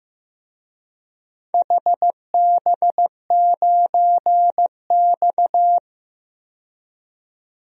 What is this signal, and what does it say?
Morse code, heard as a keyed tone, about 15 wpm, 700 Hz. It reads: HB9X
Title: DXCC-100 Call Signs - 4 Characters - 15wpm